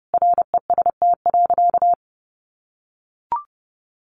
Morse code, heard as a keyed tone, about 30 wpm, 700 Hz. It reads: REST.
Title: Sentences from Top 1400 Words 30wpm